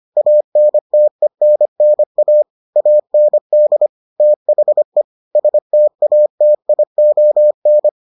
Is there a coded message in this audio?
Yes: Morse code from 0.2 to 7.9 s, heard as a keyed tone, about 25 wpm, 600 Hz. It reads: ANTENNA AND THE STATION